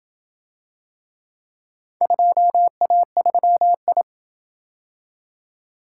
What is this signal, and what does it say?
Morse code, heard as a keyed tone, about 27 wpm, 700 Hz. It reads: 2A3I